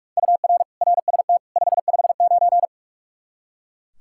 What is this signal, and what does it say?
Morse code, heard as a keyed tone, about 41 wpm, 700 Hz. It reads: UR RST 559
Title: Practice QSO - Exchange 1 - Beginning - 45wpm